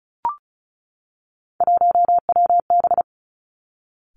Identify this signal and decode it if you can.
Morse code, heard as a keyed tone, about 36 wpm, 700 Hz. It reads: 1WB
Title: Three Challenging Letter-Number Combinations 35wpm